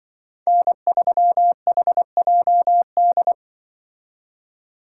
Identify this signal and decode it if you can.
Morse code, heard as a keyed tone, about 23 wpm, 700 Hz. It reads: N3HJD
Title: Speed Racing - USA 1x3 Call Signs 20wpm